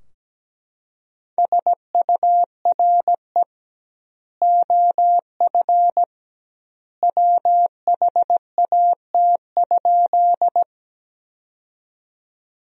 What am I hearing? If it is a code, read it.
Morse code, heard as a keyed tone, about 17 wpm, 700 Hz. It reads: SURE OF WHAT?
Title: 2x Word-Spacing - Sentences from Top 300 Words - 17wpm